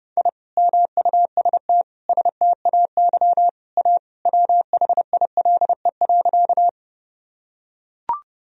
Morse code, heard as a keyed tone, about 31 wpm, 700 Hz. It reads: I MUST STAY A WHILE.
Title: Sentences from Top 1000 Words - Review Part 1 - 30wpm